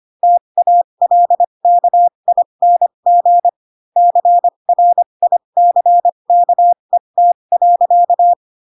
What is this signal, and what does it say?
Morse code, heard as a keyed tone, about 25 wpm, spaced as for 22 wpm, 700 Hz. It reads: TALKING CRICKET.
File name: Pinocchio / Pinocchio ch4 18 25 ik1hge